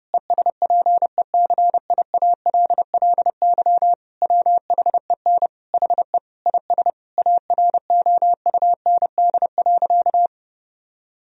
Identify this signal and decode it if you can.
Morse code, heard as a keyed tone, about 30 wpm, 700 Hz. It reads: ESPECIALLY WHEN HE IS AROUND.
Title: Sentences from Top 1000 Words - Review Part 4 - 30wpm